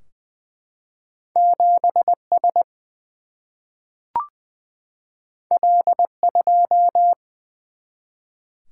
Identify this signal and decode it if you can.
Morse code, heard as a keyed tone, about 20 wpm, 700 Hz. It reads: 7 S L 2
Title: Single-Letter Single-Number 20wpm